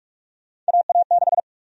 Morse code, heard as a keyed tone, about 46 wpm, 700 Hz. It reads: AA6
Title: USA Call Sign Prefixes - 45wpm